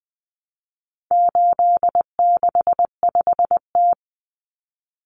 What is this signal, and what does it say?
Morse code, heard as a keyed tone, about 20 wpm, 700 Hz. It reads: 865T
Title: Common Cut Numbers - 4-Digit 20wpm